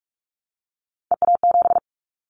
Morse code, heard as a keyed tone, about 46 wpm, 700 Hz. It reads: EA7
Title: Callsign Prefixes for Top 25 DXCC - 45wpm